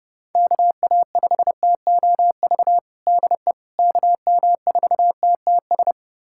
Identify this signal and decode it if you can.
Morse code, heard as a keyed tone, about 29 wpm, 700 Hz. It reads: KA5TOV DE KM4TTS
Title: USA Callsign DE Callsign - 6 Characters - 30wpm